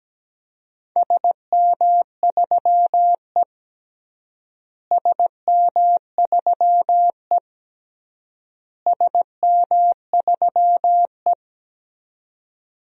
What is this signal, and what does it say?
Morse code, heard as a keyed tone, about 17 wpm, 700 Hz. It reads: SM3E SM3E SM3E
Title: DXCC-100 Call Signs - 4 Characters - Repeated 3x - 17wpm